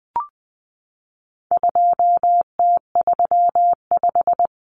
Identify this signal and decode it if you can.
Morse code, heard as a keyed tone, about 20 wpm, 700 Hz. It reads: 2T35
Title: Common Cut Numbers - 4-Digit 20wpm